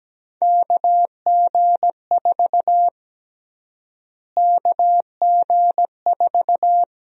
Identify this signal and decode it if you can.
Morse code, heard as a keyed tone, about 17 wpm, 700 Hz. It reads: KG4 KG4
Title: USA Call Sign Prefixes - Repeated 2x - 17wpm